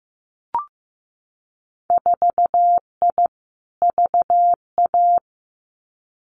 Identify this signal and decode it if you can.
Morse code, heard as a keyed tone, about 15 wpm, 700 Hz. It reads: 4I VA
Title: Winter Field Day Exchanges 15wpm